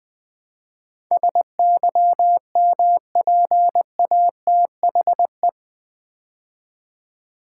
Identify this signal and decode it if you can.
Morse code, heard as a keyed tone, about 20 wpm, 700 Hz. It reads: SYMPATHE
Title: Build A Word - 4 Letters At A Time - 11-Letter Words - 20wpm